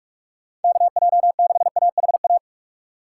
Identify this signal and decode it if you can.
Morse code, heard as a keyed tone, about 46 wpm, 700 Hz. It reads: KJ6AHA
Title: USA Call Signs - 6 Characters - 2x Character Spacing - 45wpm